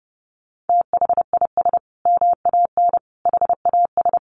Morse code, heard as a keyed tone, about 30 wpm, 700 Hz. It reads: THIS MAN HAS
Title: Sentences from Top 1300 Words 30wpm